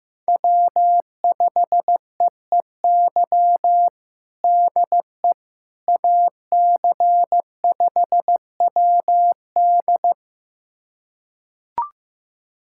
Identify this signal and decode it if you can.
Morse code, heard as a keyed tone, about 15 wpm, 700 Hz. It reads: W5EEY DE AC5WD
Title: USA Callsign DE Callsign - 5 Characters - 15wpm